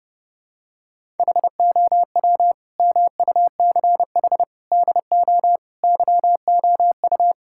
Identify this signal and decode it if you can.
Morse code, heard as a keyed tone, about 30 wpm, 700 Hz. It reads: HOW MUCH DO YOU
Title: Sentences from Top 1000 Words - Review Part 6 - 30wpm